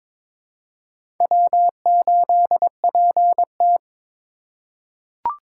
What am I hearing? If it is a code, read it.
Morse code, heard as a keyed tone, about 22 wpm, 700 Hz. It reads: W8PT
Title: USA Call Signs - 4 Characters - 4x Character Spacing - 22wpm